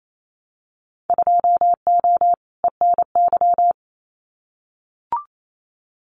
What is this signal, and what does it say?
Morse code, heard as a keyed tone, about 28 wpm, 700 Hz. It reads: 2O ENY
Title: Winter Field Day Exchanges 28wpm